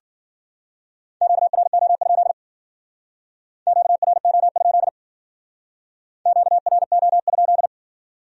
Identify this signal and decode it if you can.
Morse code, heard as a keyed tone, about 50 wpm, 700 Hz. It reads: QRO? QRO? QRO?
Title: Speed-Racing Q-Codes - 40wpm